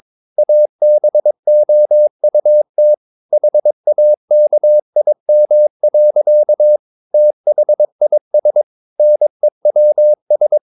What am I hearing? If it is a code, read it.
Morse code, heard as a keyed tone, about 22 wpm, 600 Hz. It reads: ABOUT HAKIM. THIS NEWS
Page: Romanen Never af Ken Follet på engelsk.